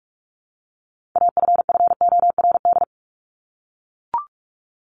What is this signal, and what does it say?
Morse code, heard as a keyed tone, about 44 wpm, 700 Hz. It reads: AFFORD
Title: Top 1001-1250 Words 45wpm